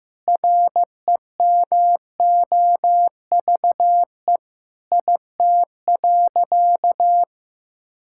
Morse code, heard as a keyed tone, about 15 wpm, 700 Hz. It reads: REMOVE IT.
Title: Sentences from Top 1000 Words - Review Part 5 - 15wpm